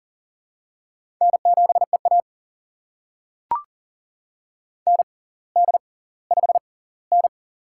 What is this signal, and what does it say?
Morse code, heard as a keyed tone, about 38 wpm, spaced as for 20 wpm, 700 Hz. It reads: N7EA ND5N
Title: USA Call Signs - 4 Characters - 6x Character Spacing - 40wpm